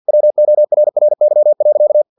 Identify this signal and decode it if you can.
Morse code, heard as a keyed tone, about 50 wpm, 600 Hz. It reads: WORRY.
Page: Never 49wpm